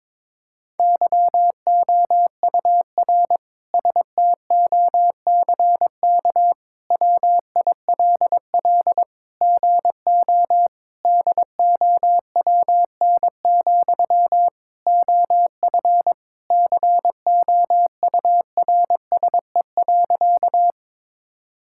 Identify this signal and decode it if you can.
Morse code, heard as a keyed tone, about 22 wpm, 700 Hz. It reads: YOUR STOCK WILL GO DOWN, OF COURSE.
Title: Sentences from Top 1000 Words - Review Part 5 - 22wpm